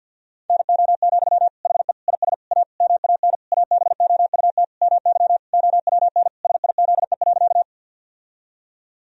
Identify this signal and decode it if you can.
Morse code, heard as a keyed tone, about 50 wpm, 700 Hz. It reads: NO, HE IS A MAN ABOUT MY OWN SIZE.
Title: Sentences from Top 1000 Words - Review Part 2 - 50wpm